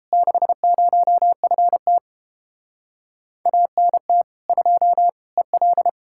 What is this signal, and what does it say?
Morse code, heard as a keyed tone, about 33 wpm, 700 Hz. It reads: 60FT ANT 2 EL